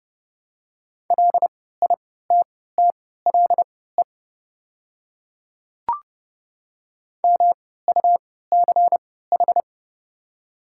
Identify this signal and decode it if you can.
Morse code, heard as a keyed tone, about 30 wpm, spaced as for 17 wpm, 700 Hz. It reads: LITTLE MUCH
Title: 3x Character Spacing - Top 200 Words - 30wpm